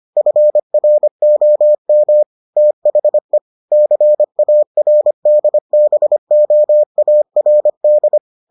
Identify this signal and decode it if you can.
Morse code, heard as a keyed tone, about 25 wpm, 600 Hz. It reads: FROM THE CARDBOARD